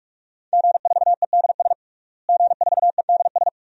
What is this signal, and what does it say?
Morse code, heard as a keyed tone, about 46 wpm, 700 Hz. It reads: G4EDS G4EDS
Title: DXCC-100 Call Signs - 5 Characters - Repeated 2x - 45wpm